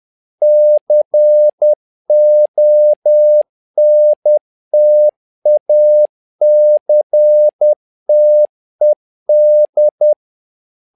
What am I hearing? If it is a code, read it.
Morse code, heard as a keyed tone, about 10 wpm, 600 Hz. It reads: CONTACTED
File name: learning center / tower trouble / tower trouble 10wpm